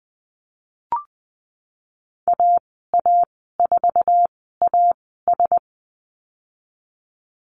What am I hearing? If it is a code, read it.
Morse code, heard as a keyed tone, about 20 wpm, 700 Hz. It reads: A A 4 A S
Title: 048 Lesson 9g - Number 4 - Call Signs 20wpm